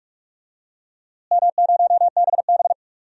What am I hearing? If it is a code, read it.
Morse code, heard as a keyed tone, about 46 wpm, 700 Hz. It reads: M0BB
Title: DXCC-100 Call Signs - 4 Characters - Repeated 3x - 45wpm